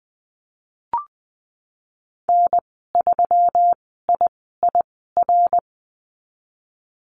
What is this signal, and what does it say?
Morse code, heard as a keyed tone, about 19 wpm, spaced as for 14 wpm, 700 Hz. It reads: N3IIR
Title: 147 Lesson 24g - Letter y - Call Signs 20wpm